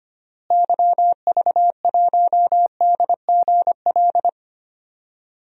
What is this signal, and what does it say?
Morse code, heard as a keyed tone, about 25 wpm, 700 Hz. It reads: YV1DGL
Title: DXCC-100 Call Signs - 6 Characters - Repeated 3x - 25wpm